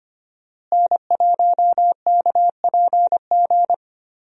Morse code, heard as a keyed tone, about 25 wpm, 700 Hz. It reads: N1KPG
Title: USA Call Signs - 5 Characters - Repeated 2x - 25wpm